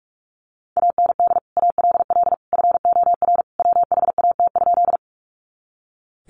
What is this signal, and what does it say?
Morse code, heard as a keyed tone, about 44 wpm, 700 Hz. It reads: AND ALL FOR WHAT?
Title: Sentences from Top 100 Words 45wpm